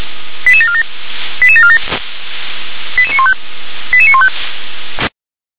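Five-tone selective call (ZVEI), heard as quick runs of five tones, tones of about 70 ms. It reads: 88747 88747 88915 88915
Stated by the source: ZVEI five-tone selective call, code 88994, received with an SDR.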